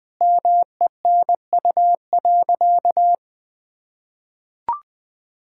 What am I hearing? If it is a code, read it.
Morse code, heard as a keyed tone, about 20 wpm, 700 Hz. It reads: MENU.
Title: Sentences with 4-Word Phrases - Spoken Beginning - 20wpm